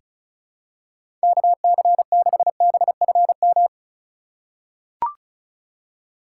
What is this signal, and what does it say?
Morse code, heard as a keyed tone, about 34 wpm, 700 Hz. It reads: KC6BFM
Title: USA Call Signs - 6 Characters - 2x Character Spacing - 35wpm